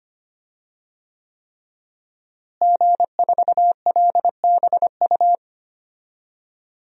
Morse code, heard as a keyed tone, about 25 wpm, 700 Hz. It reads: G4LBU
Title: DXCC-100 Call Signs - 5 Characters - Repeated 3x - 25wpm